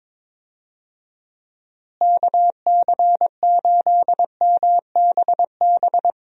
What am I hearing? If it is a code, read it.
Morse code, heard as a keyed tone, about 22 wpm, 700 Hz. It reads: KC8MBB
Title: USA Call Signs - 6 Characters - Repeated 2x - 22wpm